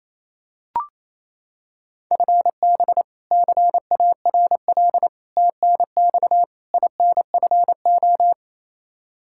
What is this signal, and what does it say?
Morse code, heard as a keyed tone, about 28 wpm, 700 Hz. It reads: FB CARL TNX INFO
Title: Practice QSO - Exchange 3 - More Info - 28wpm